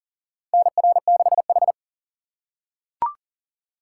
Morse code, heard as a keyed tone, about 40 wpm, 700 Hz. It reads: NR6H
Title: USA Call Signs - 4 Characters - 6x Character Spacing - 40wpm